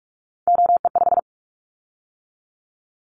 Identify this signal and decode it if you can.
Morse code, heard as a keyed tone, about 44 wpm, 700 Hz. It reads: OE5
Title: Callsign Prefixes for Top 100 DXCC - 45wpm